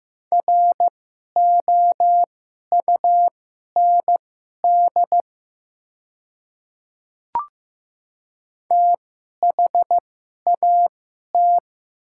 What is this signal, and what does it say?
Morse code, heard as a keyed tone, about 15 wpm, spaced as for 11 wpm, 700 Hz. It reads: ROUND THAT
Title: 2x Character Spacing - Top 200 Words - 15wpm